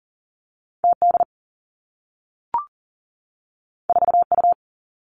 Morse code, heard as a keyed tone, about 42 wpm, 700 Hz. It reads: TD 4U
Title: Two Challenging Letter-Number Combinations 40wpm